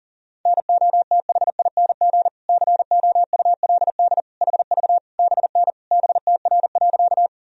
Morse code, heard as a keyed tone, about 37 wpm, 700 Hz. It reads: NOTHING COULD HV BN BTR.
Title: Sentences from Top 500 Words and Abbreviations 40wpm